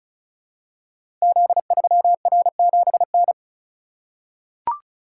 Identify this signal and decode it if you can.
Morse code, heard as a keyed tone, about 36 wpm, 700 Hz. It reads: Z3R7N
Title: Code Groups - 5 Letter:Numbers - 35wpm